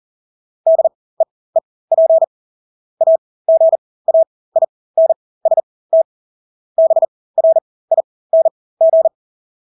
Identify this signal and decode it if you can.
Morse code, heard as a keyed tone, about 40 wpm, spaced as for 20 wpm, 650 Hz. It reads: DEEP AGAINST BRING
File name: Engelske ord 40-20 wpm